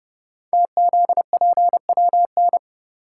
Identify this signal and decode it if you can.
Morse code, heard as a keyed tone, about 30 wpm, 700 Hz. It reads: TZPWN
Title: Code Groups - 5 Letter:Numbers - 30wpm